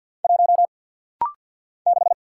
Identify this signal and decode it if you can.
Morse code, heard as a keyed tone, about 50 wpm, 700 Hz. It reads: 1 6
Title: Single Letter-Number - Rapid-Fire 50wpm